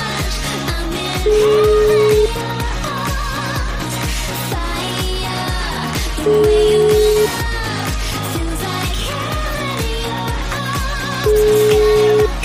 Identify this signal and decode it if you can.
Ringback tone (Europe, 425 Hz), with noise.